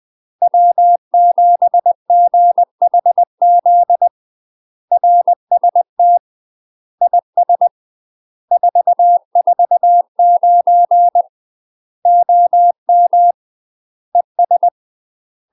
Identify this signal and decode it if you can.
Morse code, heard as a keyed tone, about 20 wpm, 700 Hz. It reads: W7GHZ RST IS 449 OM ES
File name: UK MASTER QSOs TRAINING / 20WPM / UK MASTER QSOs 0044